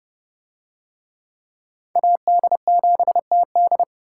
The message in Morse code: AD7TD

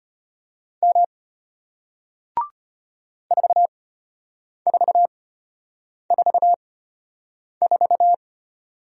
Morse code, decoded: M 4444

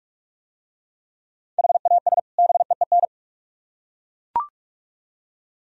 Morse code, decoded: HAS BEEN

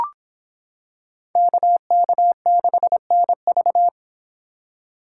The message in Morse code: KK6NV

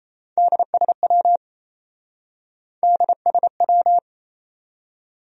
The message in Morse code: DSW DSW

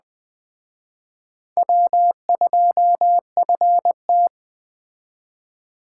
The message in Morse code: W2FT

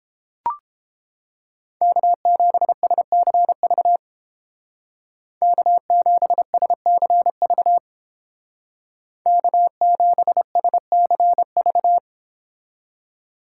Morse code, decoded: K7SCV K7SCV K7SCV

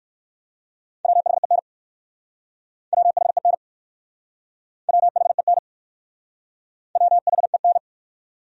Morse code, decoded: WHEN WHEN WHEN WHEN

